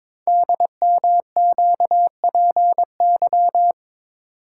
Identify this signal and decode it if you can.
Morse code, heard as a keyed tone, about 22 wpm, 700 Hz. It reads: DMQPY
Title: Code Groups - 5 Letters - 22wpm